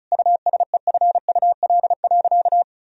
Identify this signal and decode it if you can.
Morse code, heard as a keyed tone, about 36 wpm, 700 Hz. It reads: USEFUL.
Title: Sentences from Top 1000 Words - Review Part 4 - 35wpm